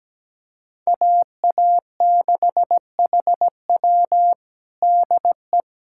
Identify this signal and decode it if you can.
Morse code, heard as a keyed tone, about 17 wpm, 700 Hz. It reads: AA6HW DE
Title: USA Callsign DE Callsign 17wpm